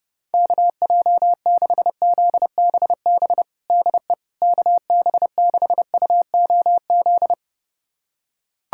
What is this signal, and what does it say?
Morse code, heard as a keyed tone, about 29 wpm, 700 Hz. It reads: KJ6ZBB DE KB6UOZ
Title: USA Callsign DE Callsign 30wpm